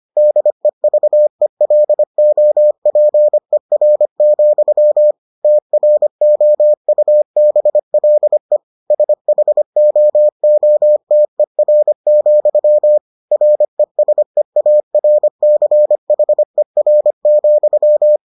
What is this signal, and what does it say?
Morse code, heard as a keyed tone, about 25 wpm, 600 Hz. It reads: DEVELOPER, TROUBLE SHOOTER, RESEARCHER,